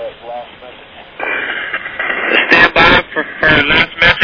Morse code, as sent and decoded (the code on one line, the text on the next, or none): none